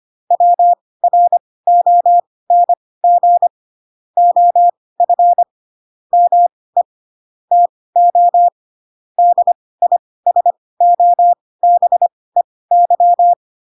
Morse code, heard as a keyed tone, about 25 wpm, spaced as for 18 wpm, 700 Hz. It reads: WRONG OF ME TO DISOBEY
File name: Pinocchio / Pinocchio ch5 12 25 ik1hge